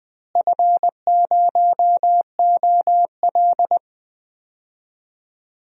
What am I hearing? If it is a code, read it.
Morse code, heard as a keyed tone, about 20 wpm, 700 Hz. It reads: F0OL
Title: DXCC-100 Call Signs - 4 Characters - Repeated 2x - 20wpm